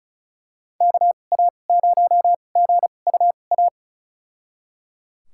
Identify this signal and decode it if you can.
Morse code, heard as a keyed tone, about 36 wpm, spaced as for 25 wpm, 700 Hz. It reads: KA0GUA
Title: USA Call Signs - 6 Characters - 2x Character Spacing - 35wpm